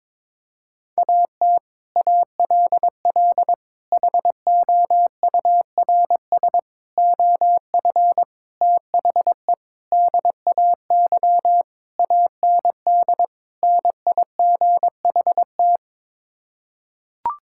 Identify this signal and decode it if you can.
Morse code, heard as a keyed tone, about 22 wpm, 700 Hz. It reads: AT ALL HOURS OF THE DAY AND NIGHT
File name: practice / 22wpm / Sets of 8 Words - 22wpm